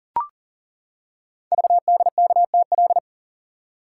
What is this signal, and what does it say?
Morse code, heard as a keyed tone, about 40 wpm, 700 Hz. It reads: VDKTL
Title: Code Groups - 5 Letters - 40wpm